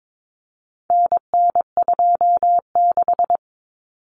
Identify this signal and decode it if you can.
Morse code, heard as a keyed tone, about 22 wpm, 700 Hz. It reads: NN26
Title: Common Cut Numbers - 4-Digit 22wpm